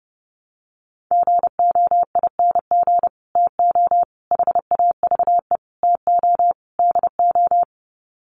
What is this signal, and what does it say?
Morse code, heard as a keyed tone, about 28 wpm, 700 Hz. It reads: GOING TO HAVE TO DO
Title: Sets of 5 Words Encore 30wpm